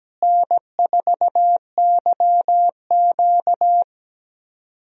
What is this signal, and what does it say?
Morse code, heard as a keyed tone, about 17 wpm, 700 Hz. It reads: N4YQ
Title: USA Call Signs - 4 Characters - Repeated 2x - 17wpm